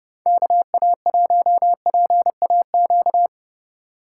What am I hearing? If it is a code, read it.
Morse code, heard as a keyed tone, about 29 wpm, 700 Hz. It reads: KA1PAQ